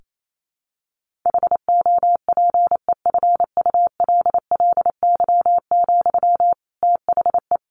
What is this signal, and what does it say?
Morse code, heard as a keyed tone, about 28 wpm, 700 Hz. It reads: HOPEFULLY, THE
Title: Sentences from Top 1500 Words 28wpm